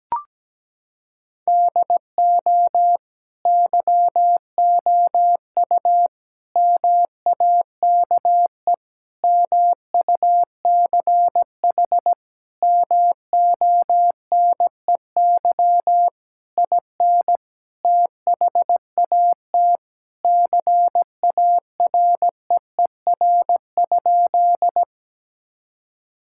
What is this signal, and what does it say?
Morse code, heard as a keyed tone, about 17 wpm, 700 Hz. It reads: DO YOU MAKE MUCH MONEY IN THAT CAREER?
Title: Sentences from Top 1000 Words - Review Part 6 - 17wpm